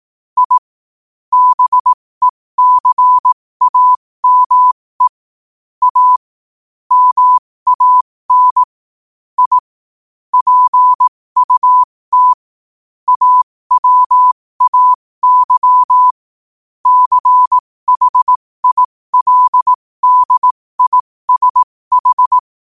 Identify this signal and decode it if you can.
Morse code, heard as a keyed tone, about 18 wpm, spaced as for 15 wpm, 1000 Hz. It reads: I BECAME A MAN I PUT AWAY CHILDISH
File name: qotdfiles / 15wpm